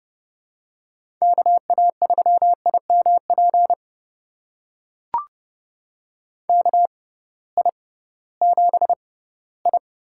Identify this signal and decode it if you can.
Morse code, heard as a keyed tone, about 31 wpm, spaced as for 18 wpm, 700 Hz. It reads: KA3IMP KI7I
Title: USA Call Signs - 6x Character Spacing - 30wpm